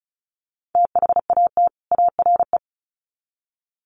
Morse code, heard as a keyed tone, about 34 wpm, 700 Hz. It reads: THAT ARE
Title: Sets of 2 Words 35wpm